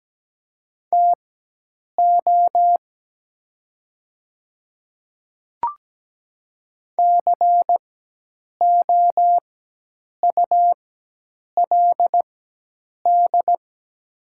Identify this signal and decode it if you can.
Morse code, heard as a keyed tone, about 17 wpm, spaced as for 8 wpm, 700 Hz. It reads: TO COULD
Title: 4x Character Spacing - Top 100 Words - 17wpm